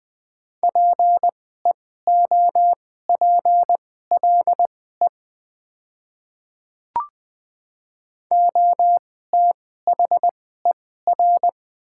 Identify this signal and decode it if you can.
Morse code, heard as a keyed tone, about 20 wpm, spaced as for 15 wpm, 690 Hz. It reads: PEOPLE OTHER
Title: 2x Character Spacing - Top 200 Words - 20wpm